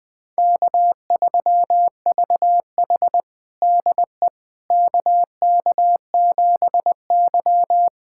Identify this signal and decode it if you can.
Morse code, heard as a keyed tone, about 20 wpm, 700 Hz. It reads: K3VH DE KK7Y
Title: USA Callsign DE Callsign - 4 Characters - 20wpm